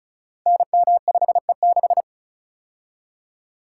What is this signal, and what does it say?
Morse code, heard as a keyed tone, about 36 wpm, 700 Hz. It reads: NM5E6
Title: Code Groups - 5 Letter:Numbers - 35wpm